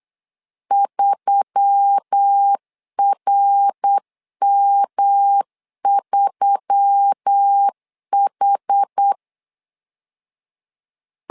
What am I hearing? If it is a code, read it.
Morse code, heard as a keyed tone, about 8 wpm, 800 Hz. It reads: 3RM3H
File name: LEZIONI / 4 Q1S3 8wpm 3037